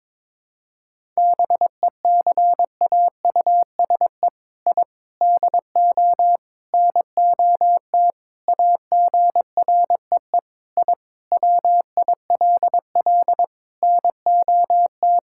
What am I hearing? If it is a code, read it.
Morse code, heard as a keyed tone, about 22 wpm, 700 Hz. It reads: BECAUSE I DO NOT AGREE I WILL NOT
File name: practice / 22wpm / Sentences with 4-Word Phrases - 22wpm